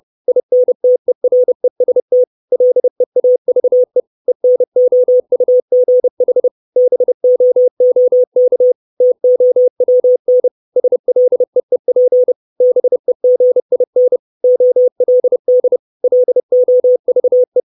Morse code, heard as a keyed tone, about 30 wpm, 500 Hz. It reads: INTEREST LEAVE ENOUGH BOOK TOWN SLEEP BEGIN OLD LOVE
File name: 632 Words 30WPM